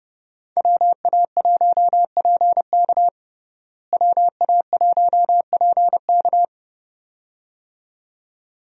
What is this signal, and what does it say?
Morse code, heard as a keyed tone, about 30 wpm, 700 Hz. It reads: WA1PK WA1PK
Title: USA Call Signs - 5 Characters - Repeated 2x - 30wpm